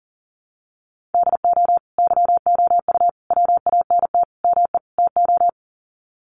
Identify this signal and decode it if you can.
Morse code, heard as a keyed tone, about 40 wpm, 700 Hz. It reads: DO YOU WANT ME TO
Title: Sets of 5 Words 40wpm